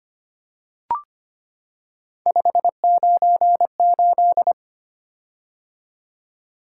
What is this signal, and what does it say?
Morse code, heard as a keyed tone, about 25 wpm, 700 Hz. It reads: 598
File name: Three Letter-Number Combinations - 25wpm